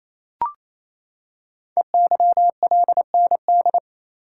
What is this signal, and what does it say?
Morse code, heard as a keyed tone, about 27 wpm, 700 Hz. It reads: EYLND
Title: Code Groups - 5 Letters - 28wpm